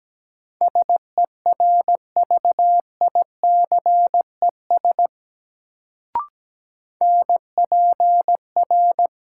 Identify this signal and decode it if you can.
Morse code, heard as a keyed tone, about 17 wpm, 700 Hz. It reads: SERVICES NPR
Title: Sets of 5 Words - New Word Drill - 17wpm